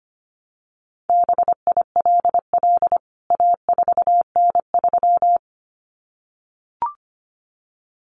Morse code, heard as a keyed tone, about 25 wpm, 700 Hz. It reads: BILL A4N3
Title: CWT Exchange 25wpm